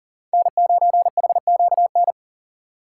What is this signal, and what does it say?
Morse code, heard as a keyed tone, about 40 wpm, 700 Hz. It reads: N9HQN
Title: USA Call Signs - 2x Character Spacing - 40wpm